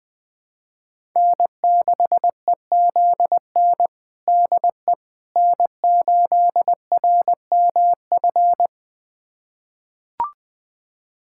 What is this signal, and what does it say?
Morse code, heard as a keyed tone, about 21 wpm, 700 Hz. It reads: N6EZN DE N8RMF
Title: USA Callsign DE Callsign - 5 Characters - 20wpm